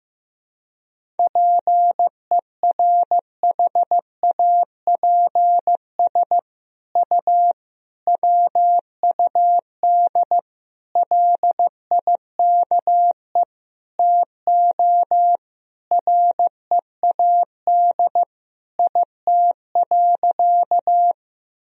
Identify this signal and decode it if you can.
Morse code, heard as a keyed tone, about 15 wpm, 700 Hz. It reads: PERHAPS U WUD LIKE TO READ IT.
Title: Sentences from Top 500 Words and Abbreviations 15wpm